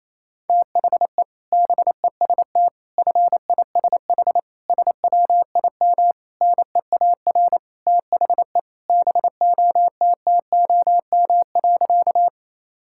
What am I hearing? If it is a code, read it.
Morse code, heard as a keyed tone, about 28 wpm, 700 Hz. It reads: THE BEST FISH SWIM NEAR THE BOTTOM.
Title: Sentences from Top 1000 Words - Review Part 6 - 28wpm